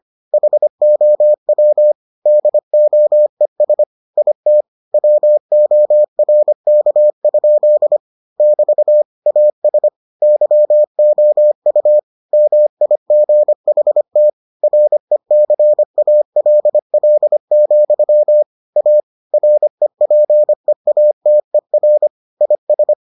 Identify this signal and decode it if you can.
Morse code, heard as a keyed tone, about 25 wpm, 600 Hz. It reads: HOW DOES IT WORK? <BT> AS YOU MIGHT RECALL, A REPEATER IS